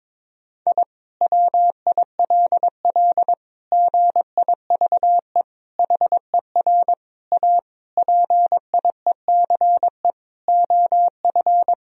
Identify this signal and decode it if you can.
Morse code, heard as a keyed tone, about 22 wpm, 700 Hz. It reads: I WILL GIVE HER A PIECE OF
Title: Sentences from Top 1000 Words - Review Part 2 - 22wpm